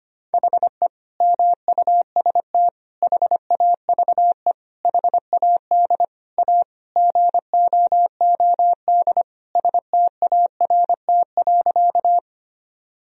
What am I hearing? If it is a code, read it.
Morse code, heard as a keyed tone, about 25 wpm, 700 Hz. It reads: HE MUST HAVE HAD A GOOD START.
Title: Sentences from Top 1000 Words - Review Part 1 - 25wpm